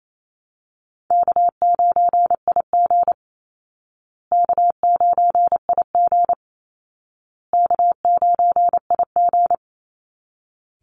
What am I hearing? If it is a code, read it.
Morse code, heard as a keyed tone, about 28 wpm, 700 Hz. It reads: K9IG K9IG K9IG
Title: Call Signs Repeated 2x - 28wpm